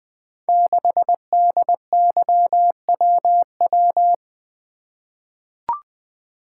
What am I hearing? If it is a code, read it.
Morse code, heard as a keyed tone, about 20 wpm, 700 Hz. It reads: 6DYWW
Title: Code Groups - 5 Letter:Numbers - 20wpm